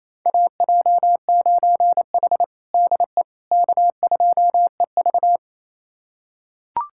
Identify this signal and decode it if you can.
Morse code, heard as a keyed tone, about 28 wpm, 700 Hz. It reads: AJ9H DE K2EV
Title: USA Callsign DE Callsign - 4 Characters - 28wpm